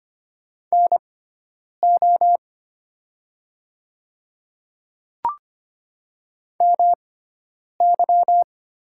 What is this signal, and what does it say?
Morse code, heard as a keyed tone, about 25 wpm, spaced as for 9 wpm, 700 Hz. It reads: NO MY
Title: 6x Character Spacing - Top 100 Words - 25wpm